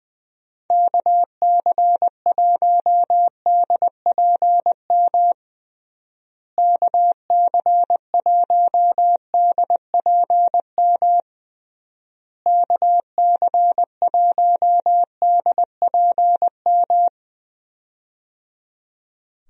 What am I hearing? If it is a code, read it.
Morse code, heard as a keyed tone, about 21 wpm, 700 Hz. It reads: KC1DPM KC1DPM KC1DPM
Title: USA Call Signs - 6 Characters - Repeated 3x - 20wpm